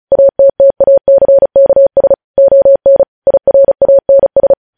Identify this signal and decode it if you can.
Morse code, heard as a keyed tone, about 36 wpm, 550 Hz. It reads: ATTACKS ON IRANS
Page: news headlines in morse code at 35 wpm